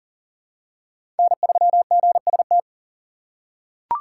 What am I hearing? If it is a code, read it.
Morse code, heard as a keyed tone, about 40 wpm, 700 Hz. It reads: N3GST
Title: USA Call Signs - 5 Characters - 2x Character Spacing - 40wpm